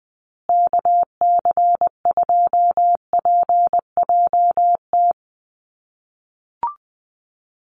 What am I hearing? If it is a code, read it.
Morse code, heard as a keyed tone, about 20 wpm, 700 Hz. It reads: KC2PJT
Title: USA Call Signs Repeated 2x 20wpm